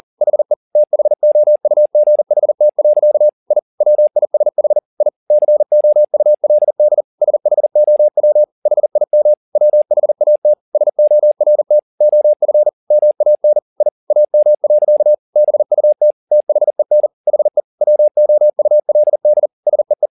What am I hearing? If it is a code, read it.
Morse code, heard as a keyed tone, about 40 wpm, 600 Hz. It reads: HE THOUGHT. I WISH I COULD SHOW HIM WHAT SORT OF MAN I AM. BUT THEN HE WOULD SEE